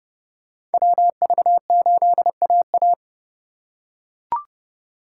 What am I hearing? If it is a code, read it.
Morse code, heard as a keyed tone, about 28 wpm, 700 Hz. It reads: WV8AA